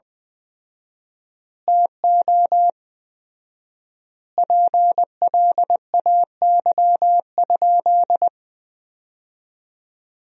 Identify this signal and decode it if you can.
Morse code, heard as a keyed tone, about 20 wpm, 700 Hz. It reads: TO PLAY?
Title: 4x Word-Spacing - Sentences from Top 400 Words - 20wpm